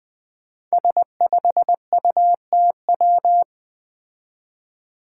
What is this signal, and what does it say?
Morse code, heard as a keyed tone, about 20 wpm, 700 Hz. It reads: S5UTW
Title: Five Challenging Letter-Number Combinations 20wpm